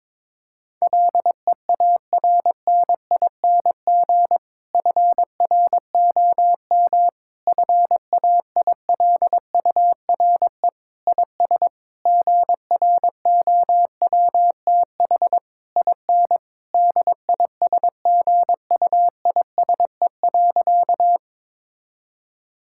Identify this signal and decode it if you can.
Morse code, heard as a keyed tone, about 22 wpm, 700 Hz. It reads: LEARNING FROM FAILURE IS GROWTH IN DISGUISE.
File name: Difficult Sentences with Hints - 22wpm